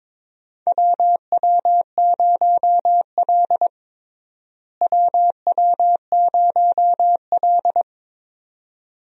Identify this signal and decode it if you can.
Morse code, heard as a keyed tone, about 22 wpm, 700 Hz. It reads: WW0L WW0L
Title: USA Call Signs - 4 Characters - Repeated 2x - 22wpm